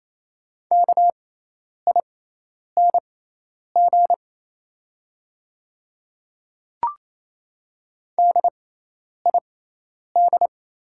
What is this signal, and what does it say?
Morse code, heard as a keyed tone, about 28 wpm, spaced as for 10 wpm, 690 Hz. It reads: KING DID
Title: 6x Character Spacing - Top 500 Words - 28wpm